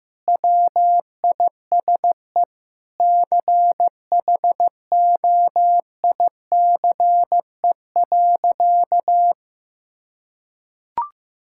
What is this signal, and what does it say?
Morse code, heard as a keyed tone, about 15 wpm, 700 Hz. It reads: WISE CHOICE.
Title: Sentences from Top 1600 Words 15wpm